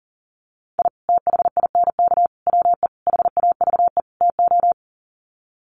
Morse code, heard as a keyed tone, about 40 wpm, 700 Hz. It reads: I THINK WE HAVE TO
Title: Sets of 5 Words 40wpm